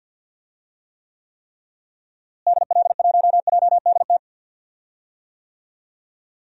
Morse code, heard as a keyed tone, about 47 wpm, 700 Hz. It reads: DL1JDT
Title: DXCC-100 Call Signs - 6 Characters - Repeated 3x - 50wpm